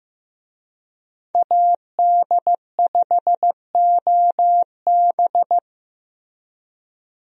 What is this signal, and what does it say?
Morse code, heard as a keyed tone, about 15 wpm, 700 Hz. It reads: AD5OB
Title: USA Call Signs - 5 Characters - Repeated 2x - 15wpm